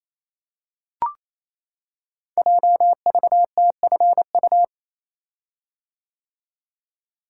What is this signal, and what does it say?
Morse code, heard as a keyed tone, about 29 wpm, 700 Hz. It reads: JVTFU
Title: Code Groups - 5 Letter:Numbers - 28wpm